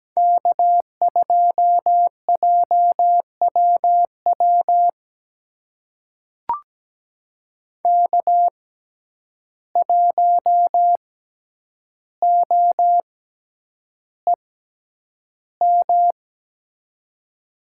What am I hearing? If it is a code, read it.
Morse code, heard as a keyed tone, about 17 wpm, spaced as for 9 wpm, 700 Hz. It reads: K2JWW K1OEM
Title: USA Call Signs - 5 Characters - 6x Character Spacing - 17wpm